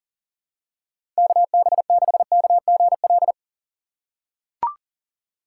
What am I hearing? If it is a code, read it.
Morse code, heard as a keyed tone, about 40 wpm, 700 Hz. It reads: KB6KGL
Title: USA Call Signs - 6 Characters - 6x Character Spacing - 40wpm